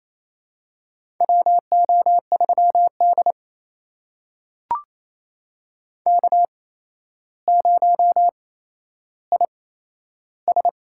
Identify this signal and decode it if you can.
Morse code, heard as a keyed tone, about 28 wpm, spaced as for 12 wpm, 700 Hz. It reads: WO3D K0IS
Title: USA Call Signs - 4 Characters - 8x Character Spacing - 28wpm